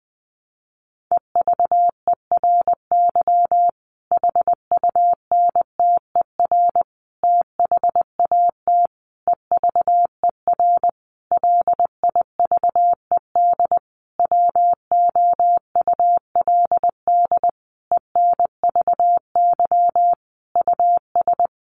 Morse code, heard as a keyed tone, about 20 wpm, 700 Hz. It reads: EVERY HUNTER THAT EVER LIVED WOULD ENVY US
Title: A Sound Of Thunder 20wpm